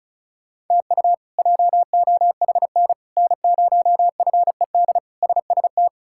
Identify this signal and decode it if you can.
Morse code, heard as a keyed tone, about 34 wpm, 700 Hz. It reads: TU JOHN N0FED SST
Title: K1USN SST QSO Practice - 35wpm